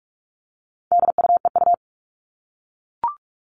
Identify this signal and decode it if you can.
Morse code, heard as a keyed tone, about 44 wpm, 700 Hz. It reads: DUEU
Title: All Cut Numbers - 4-Digit 45wpm